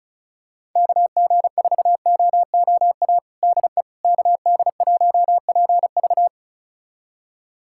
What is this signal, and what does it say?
Morse code, heard as a keyed tone, about 36 wpm, 700 Hz. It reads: KG4OOA DE KD1PV
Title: USA Callsign DE Callsign 35wpm